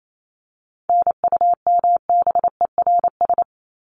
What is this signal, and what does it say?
Morse code, heard as a keyed tone, about 28 wpm, 700 Hz. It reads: NUMBERS
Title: ES:S Suffix - Top 100 Words with Base Words 28wpm